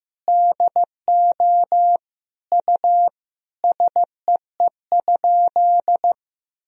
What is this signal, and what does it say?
Morse code, heard as a keyed tone, about 15 wpm, 700 Hz. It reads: DO U SEE?
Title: Sentences from Top 100 Words and Abbreviations 15wpm